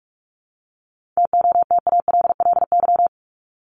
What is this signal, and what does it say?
Morse code, heard as a keyed tone, about 44 wpm, 700 Hz. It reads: TOTALLY
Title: Top 1501-1750 Words 45wpm